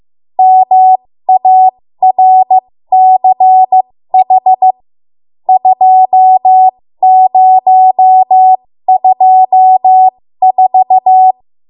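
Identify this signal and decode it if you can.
Morse code, heard as a keyed tone, about 15 wpm, spaced as for 13 wpm, 750 Hz. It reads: MARCH 2024
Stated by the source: 13 wpm Morse code practice archive transmission.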